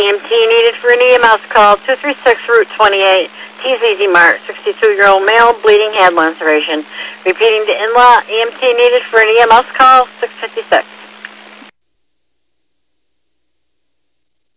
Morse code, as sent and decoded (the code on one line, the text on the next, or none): none